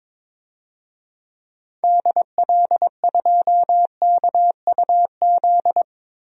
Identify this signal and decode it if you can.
Morse code, heard as a keyed tone, about 22 wpm, 700 Hz. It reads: DL2KUZ
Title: DXCC-100 Call Signs - 6 Characters - 22wpm